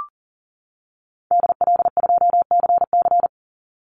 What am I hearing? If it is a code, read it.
Morse code, heard as a keyed tone, about 41 wpm, 700 Hz. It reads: DL2CC
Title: Call Signs - 40wpm